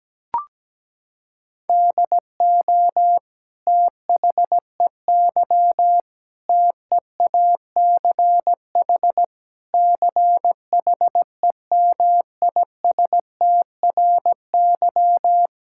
Morse code, heard as a keyed tone, about 17 wpm, 700 Hz. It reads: DO THEY TEACH CHEMISTRY
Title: Sentences from Top 1000 Words - Review Part 6 - 17wpm